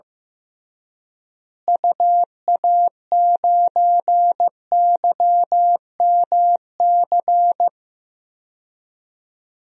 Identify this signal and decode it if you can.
Morse code, heard as a keyed tone, about 15 wpm, 700 Hz. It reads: UA9YMC
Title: DXCC-100 Call Signs - 6 Characters - Repeated 2x - 15wpm